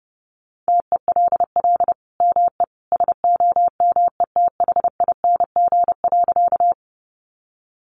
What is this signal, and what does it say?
Morse code, heard as a keyed tone, about 30 wpm, 700 Hz. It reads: TELL ME SOMETHING.
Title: Sentences from Top 600 Words 30wpm